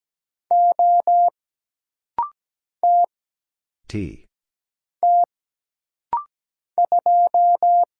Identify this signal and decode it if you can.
Morse code, heard as a keyed tone, about 17 wpm, 700 Hz. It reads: O T T 2
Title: Single Letter-Number - Rapid-Fire 17wpm